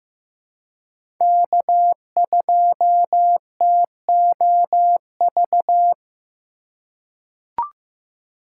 Morse code, heard as a keyed tone, about 15 wpm, 700 Hz. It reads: K2TOV
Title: USA Call Signs - 5 Characters - 3x Character Spacing - 15wpm